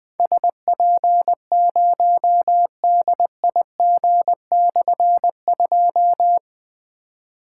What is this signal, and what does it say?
Morse code, heard as a keyed tone, about 21 wpm, 700 Hz. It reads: SP0DIG/2
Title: Difficult Call Signs 20wpm